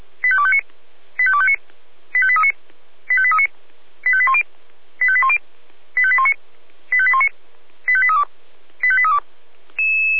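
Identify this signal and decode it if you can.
Five-tone selective call (ZVEI), heard as quick runs of five tones, tones of about 70 ms. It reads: 85259 85259 85839 85839 85810 85810 85819 85819 85832 85832
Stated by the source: ZVEI five-tone call received by SDR.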